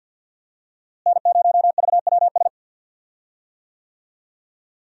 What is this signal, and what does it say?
Morse code, heard as a keyed tone, about 46 wpm, 700 Hz. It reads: N0VWS
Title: USA Call Signs - 5 Characters - Repeated 3x - 50wpm